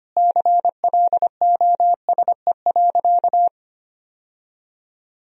Morse code, heard as a keyed tone, about 25 wpm, 700 Hz. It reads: CLOSE.
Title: Sentences from Top 1000 Words - Review Part 1 - 25wpm